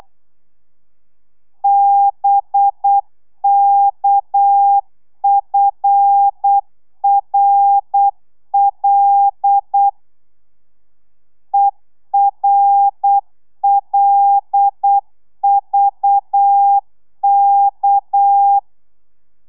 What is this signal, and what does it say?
Morse code, heard as a keyed tone, about 8 wpm, 800 Hz. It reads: BKFRL ERLVK
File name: telegrafia / i2tfj / Lez6 35